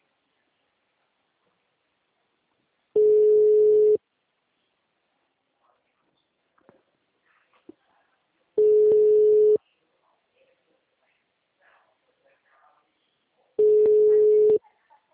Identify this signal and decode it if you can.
Ringback tone (Europe, 425 Hz), with speech.
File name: out- -2004- - - .14654